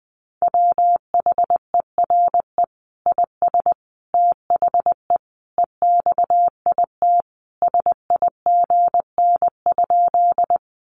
Morse code, heard as a keyed tone, about 20 wpm, 700 Hz. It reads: WHERE IS THE EXIT SIGN?
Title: Sentences from Top 1100 Words 20wpm